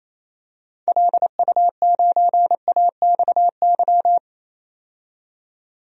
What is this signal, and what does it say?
Morse code, heard as a keyed tone, about 27 wpm, 700 Hz. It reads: LU9AXY
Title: DXCC-100 Call Signs - 6 Characters - 28wpm